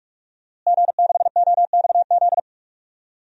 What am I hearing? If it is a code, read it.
Morse code, heard as a keyed tone, about 47 wpm, 700 Hz. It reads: G6OXZ
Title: DXCC-100 Call Signs - 5 Characters - Repeated 3x - 45wpm